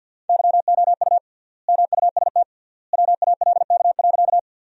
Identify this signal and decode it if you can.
Morse code, heard as a keyed tone, about 47 wpm, 700 Hz. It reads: YOU MUST WALK.
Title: 3x Word-Spacing - Sentences from Top 300 Words - 50wpm